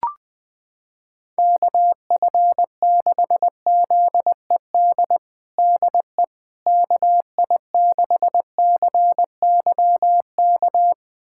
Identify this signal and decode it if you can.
Morse code, heard as a keyed tone, about 21 wpm, 700 Hz. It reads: KF6ZED DE KI6CYK
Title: USA Callsign DE Callsign - 6 Characters - 20wpm